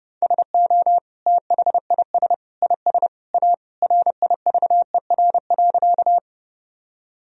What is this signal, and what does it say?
Morse code, heard as a keyed tone, about 30 wpm, 700 Hz. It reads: SO THIS IS A RIVER.
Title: Sentences from Top 1000 Words - Review Part 1 - 30wpm